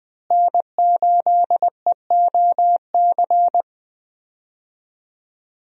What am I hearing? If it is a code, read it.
Morse code, heard as a keyed tone, about 21 wpm, 700 Hz. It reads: N8EOC
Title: USA Call Signs - 5 Characters - Repeated 2x - 20wpm